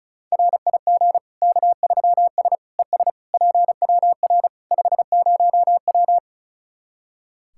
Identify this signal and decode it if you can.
Morse code, heard as a keyed tone, about 34 wpm, 700 Hz. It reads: RIG K3S ES PWR 50W
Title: K1USN SST QSO Practice 35wpm